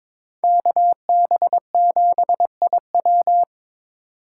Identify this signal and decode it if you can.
Morse code, heard as a keyed tone, about 26 wpm, 700 Hz. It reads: KB7IW
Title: Speed Racing - USA 2x2 Call Signs 22wpm